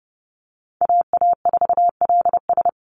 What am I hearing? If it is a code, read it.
Morse code, heard as a keyed tone, about 30 wpm, 700 Hz. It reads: AA4LS
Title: Call Signs Repeated 2x - 30wpm